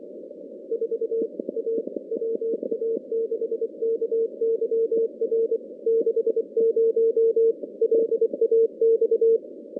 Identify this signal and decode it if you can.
Morse code, heard as a keyed tone, about 25 wpm, 450 Hz. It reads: 4 A WA6KYR 60 LAX
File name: contests / CW-SS-k5tr- -2100